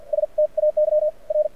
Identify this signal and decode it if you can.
Morse code, heard as a keyed tone, about 50 wpm, spaced as for 42 wpm, 600 Hz. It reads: STAY R
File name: archived / CRSnet